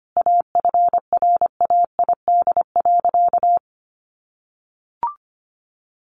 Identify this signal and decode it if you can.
Morse code, heard as a keyed tone, about 25 wpm, 700 Hz. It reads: AFRAID.
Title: Sentences from Top 1500 Words 25wpm